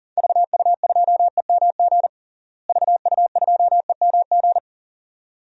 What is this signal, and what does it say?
Morse code, heard as a keyed tone, about 40 wpm, 700 Hz. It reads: VU2EMG VU2EMG
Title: DXCC-100 Call Signs - 6 Characters - Repeated 2x - 40wpm